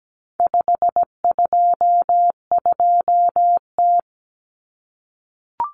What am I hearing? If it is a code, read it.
Morse code, heard as a keyed tone, about 17 wpm, 700 Hz. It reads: 522T
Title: Common Cut Numbers - 4-Digit 17wpm